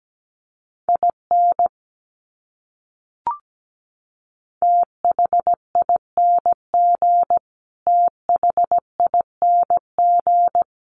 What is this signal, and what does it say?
Morse code, heard as a keyed tone, about 17 wpm, 700 Hz. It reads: IN THING THING
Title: Top 200 Words Repeated 17wpm